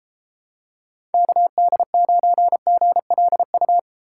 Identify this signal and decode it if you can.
Morse code, heard as a keyed tone, about 32 wpm, 700 Hz. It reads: KD9GLU